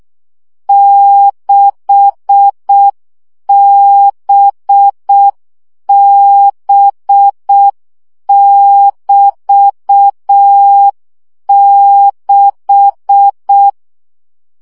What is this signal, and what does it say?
Morse code, heard as a keyed tone, about 6 wpm, 800 Hz. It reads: 6BB<BT>6
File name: CW gyakorlas b 6 egyenlo minusz 5 baud